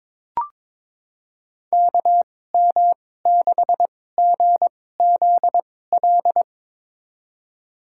Morse code, heard as a keyed tone, about 22 wpm, spaced as for 16 wpm, 700 Hz. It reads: KM6GZL
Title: USA Call Signs - 2x Character Spacing - 22wpm